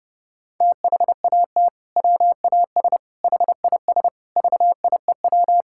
Morse code, heard as a keyed tone, about 30 wpm, 700 Hz. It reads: THAT WAS HIS VIEW
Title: Sentences from Top 1000 Words - Review Part 6 - 30wpm